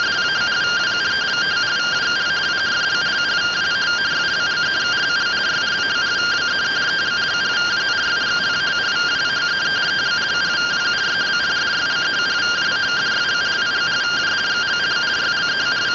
Two-tone teleprinter signal, about 170 Hz shift.
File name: findings / CT AXIAL -142.gen-19 rtty rx